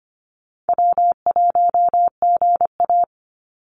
Morse code, heard as a keyed tone, about 25 wpm, 700 Hz. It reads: W1GA
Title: USA Call Signs 25wpm